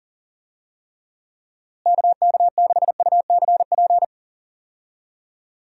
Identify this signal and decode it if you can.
Morse code, heard as a keyed tone, about 40 wpm, 700 Hz. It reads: KK6UCP
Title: USA Call Signs - 6 Characters - Repeated 2x - 40wpm